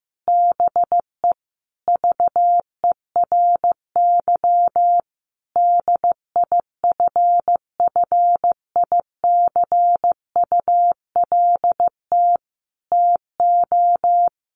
Morse code, heard as a keyed tone, about 15 wpm, 700 Hz. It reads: BE VERY DIFFICULT TO
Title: Sentences from Top 1100 Words 15wpm